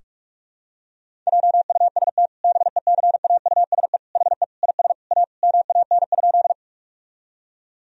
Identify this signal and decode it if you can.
Morse code, heard as a keyed tone, about 46 wpm, 700 Hz. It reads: JUST BECAUSE HE IS A MAN?
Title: Sentences from Top 1000 Words - Review Part 3 - 45wpm